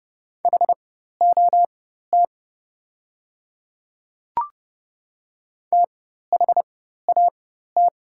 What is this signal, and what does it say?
Morse code, heard as a keyed tone, about 30 wpm, spaced as for 14 wpm, 700 Hz. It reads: HOT THAT
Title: 4x Character Spacing - Top 100 Words - 30wpm